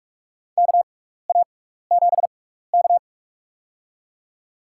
Morse code, heard as a keyed tone, about 46 wpm, spaced as for 15 wpm, 700 Hz. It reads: KA7K
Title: USA Call Signs - 6x Character Spacing - 45wpm